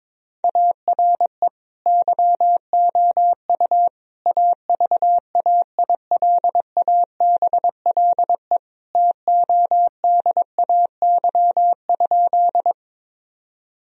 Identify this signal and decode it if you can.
Morse code, heard as a keyed tone, about 22 wpm, 700 Hz. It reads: ARE YOU AVAILABLE TODAY?
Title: Sentences from Top 1000 Words - Review Part 3 - 22wpm